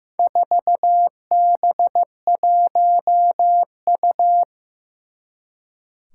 Morse code, heard as a keyed tone, about 15 wpm, 700 Hz. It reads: 4B1U
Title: Four Challenging Letter-Number Combinations 15wpm